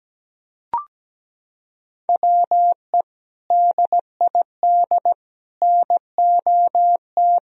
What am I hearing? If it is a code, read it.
Morse code, heard as a keyed tone, about 17 wpm, 700 Hz. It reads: WE DID NOT
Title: Sentences from Top 1000 Words - Review Part 4 - 17wpm